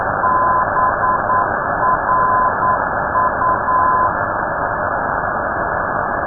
Morse code, heard as a keyed tone, about 9 wpm, 1020 Hz. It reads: BRU
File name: europe / d-3079-cw-id-2005-rb